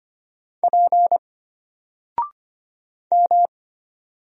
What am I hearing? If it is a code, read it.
Morse code, heard as a keyed tone, about 25 wpm, 700 Hz. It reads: P M
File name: Single-Letter-Number-25wpm